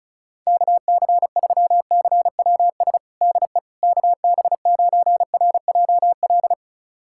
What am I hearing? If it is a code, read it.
Morse code, heard as a keyed tone, about 36 wpm, 700 Hz. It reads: KC3CWS DE KB9RJL
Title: USA Callsign DE Callsign - 6 Characters - 35wpm